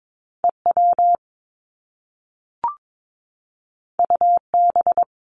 Morse code, heard as a keyed tone, about 22 wpm, 700 Hz. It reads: EW UB
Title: Top Two Letter Combinations in Words 22wpm